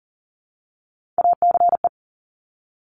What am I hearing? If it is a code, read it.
Morse code, heard as a keyed tone, about 41 wpm, 690 Hz. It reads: ACE
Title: Top Three Letter Combinations in Words 40wpm